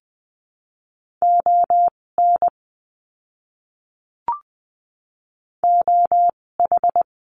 Morse code, heard as a keyed tone, about 20 wpm, spaced as for 16 wpm, 700 Hz. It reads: ON OH
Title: 256 Lesson 42a - Words 12-20wpm